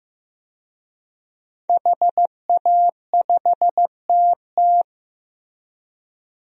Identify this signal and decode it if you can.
Morse code, heard as a keyed tone, about 15 wpm, 700 Hz. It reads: HA5TT
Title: DXCC-100 Call Signs - 5 Characters - Repeated 3x - 15wpm